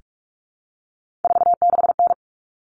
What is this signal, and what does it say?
Morse code, heard as a keyed tone, about 44 wpm, 700 Hz. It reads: 46N
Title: Common Cut Numbers - 3-Digit 45wpm